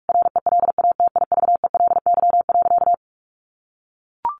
Morse code, heard as a keyed tone, about 44 wpm, 700 Hz. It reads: RELATIVELY.
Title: Sentences from Top 1100 Words 45wpm